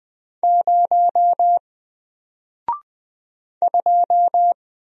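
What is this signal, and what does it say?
Morse code, heard as a keyed tone, about 20 wpm, 700 Hz. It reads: 0 2
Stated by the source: Single letter and number Morse drill at 20 wpm, from N4LCD.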